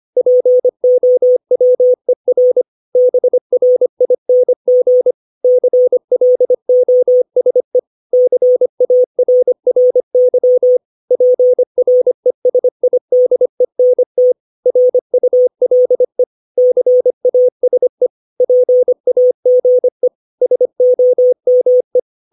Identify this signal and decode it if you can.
Morse code, heard as a keyed tone, about 25 wpm, 500 Hz. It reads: POWER BRING CLOSE CARRY PRESIDENT RULE CASE PAGE SOME
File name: 632 words / 632 Words 25WPM